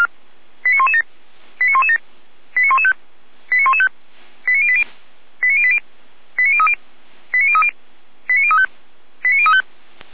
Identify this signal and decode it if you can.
Five-tone selective call (ZVEI), heard as quick runs of five tones, tones of about 80 ms. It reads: 89117 89117 89115 89115 89080 89080 89030 89030 89035 89035